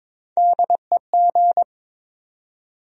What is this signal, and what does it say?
Morse code, heard as a keyed tone, about 22 wpm, 700 Hz. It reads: DEG
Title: USA Call Sign Postfixes - 3 Letters 22wpm